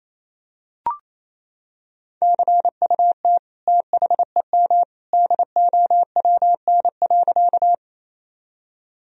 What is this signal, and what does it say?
Morse code, heard as a keyed tone, about 27 wpm, 700 Hz. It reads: CUT THEM DOWN.
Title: Sentences from Top 1000 Words - Review Part 1 - 28wpm